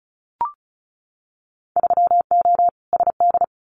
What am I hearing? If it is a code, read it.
Morse code, heard as a keyed tone, about 36 wpm, 690 Hz. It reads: 3O SD
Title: Winter Field Day Exchanges 35wpm